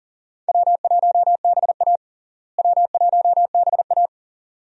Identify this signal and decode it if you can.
Morse code, heard as a keyed tone, about 40 wpm, 700 Hz. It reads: W1BA W1BA
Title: USA Call Signs - 4 Characters - Repeated 2x - 40wpm